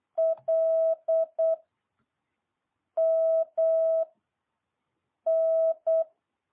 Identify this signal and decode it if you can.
Morse code, heard as a keyed tone, about 8 wpm, 650 Hz. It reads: L M N
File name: jh8cbh